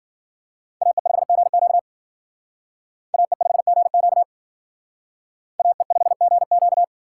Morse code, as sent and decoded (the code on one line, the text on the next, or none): .- . ..... --. --.- / .- . ..... --. --.- / .- . ..... --. --.-
AE5GQ AE5GQ AE5GQ